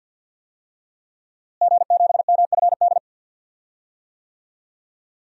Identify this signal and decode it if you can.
Morse code, heard as a keyed tone, about 50 wpm, 700 Hz. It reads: G7MFD
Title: DXCC-100 Call Signs - 5 Characters - Repeated 3x - 50wpm